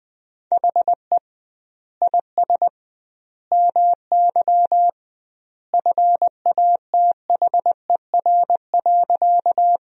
Morse code, heard as a keyed tone, about 20 wpm, 700 Hz. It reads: HE IS MY FATHER.
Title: 2x Word-Spacing - Sentences from Top 300 Words - 20wpm